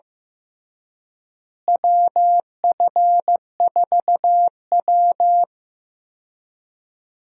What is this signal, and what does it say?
Morse code, heard as a keyed tone, about 15 wpm, 700 Hz. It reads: WF4W
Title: USA Call Signs - 4 Characters - Repeated 2x - 15wpm